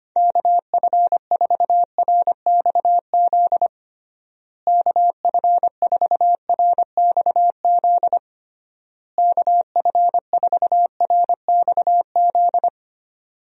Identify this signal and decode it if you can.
Morse code, heard as a keyed tone, about 25 wpm, 700 Hz. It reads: KF4RXZ KF4RXZ KF4RXZ
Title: USA Call Signs - 6 Characters - Repeated 3x - 25wpm